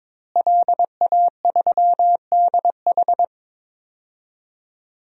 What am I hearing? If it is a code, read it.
Morse code, heard as a keyed tone, about 22 wpm, 700 Hz. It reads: LA3DH
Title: DXCC-100 Call Signs - 5 Characters - Repeated 2x - 22wpm